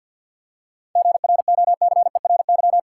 Morse code, heard as a keyed tone, about 50 wpm, 700 Hz. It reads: GROCERY
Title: Seven Letter Words - 50wpm